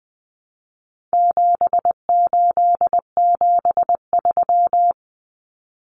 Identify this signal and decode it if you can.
Morse code, heard as a keyed tone, about 20 wpm, 700 Hz. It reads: 7873
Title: Common Cut Numbers - 4-Digit 20wpm